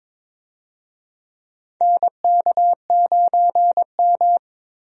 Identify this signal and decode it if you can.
Morse code, heard as a keyed tone, about 22 wpm, 700 Hz. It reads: NK9M
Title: USA Call Signs - 4 Characters - 22wpm